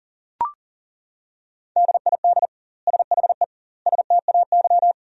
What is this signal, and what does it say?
Morse code, heard as a keyed tone, about 40 wpm, 700 Hz. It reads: DID SHE STAY
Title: 2x Word-Spacing - Sentences from Top 500 Words - 40wpm